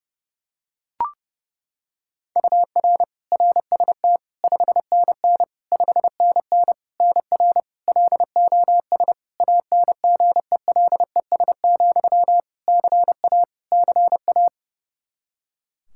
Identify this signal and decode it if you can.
Morse code, heard as a keyed tone, about 31 wpm, 700 Hz. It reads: UR RST 5NN 5NN NR LOS ANGELES, CA CA
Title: K1USN SST QSO Practice 30wpm